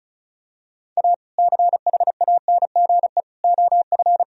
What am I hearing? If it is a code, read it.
Morse code, heard as a keyed tone, about 34 wpm, 700 Hz. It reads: A CHANGE OF
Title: Sentences from Top 1000 Words - Review Part 2 - 35wpm